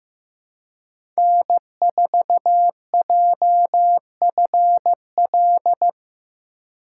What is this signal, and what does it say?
Morse code, heard as a keyed tone, about 15 wpm, 700 Hz. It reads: N4JFL
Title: USA Call Signs - 5 Characters - 3x Character Spacing - 15wpm